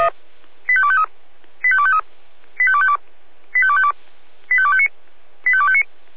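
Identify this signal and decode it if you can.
Five-tone selective call (ZVEI), heard as quick runs of five tones, tones of about 70 ms. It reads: 85253 85253 85252 85252 85259 85259
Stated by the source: ZVEI five-tone call 85250 received by SDR, from feuerwehr-orlamuende.org (fire brigade Orlamünde).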